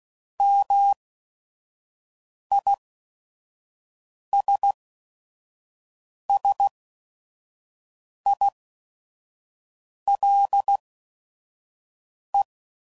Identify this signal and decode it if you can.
Morse code, heard as a keyed tone, about 16 wpm, spaced as for 5 wpm, 800 Hz. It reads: MISSILE
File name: Mixed / Misc / Words